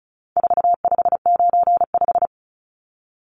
Morse code, heard as a keyed tone, about 34 wpm, 700 Hz. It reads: 4595
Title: Numbers - 4-Digit 35wpm